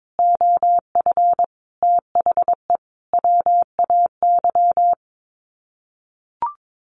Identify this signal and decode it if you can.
Morse code, heard as a keyed tone, about 22 wpm, 700 Hz. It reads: OF THE WAY
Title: Sets of 5 Words Encore2 22wpm